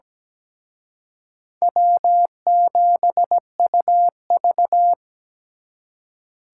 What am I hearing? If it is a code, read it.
Morse code, heard as a keyed tone, about 20 wpm, 700 Hz. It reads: W7UV
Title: Speed Racing - USA Call Signs 17wpm